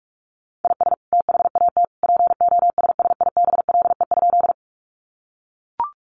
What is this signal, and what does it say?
Morse code, heard as a keyed tone, about 44 wpm, 700 Hz. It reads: IS THAT POSSIBLE?
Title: Sentences from Top 500 Words 45wpm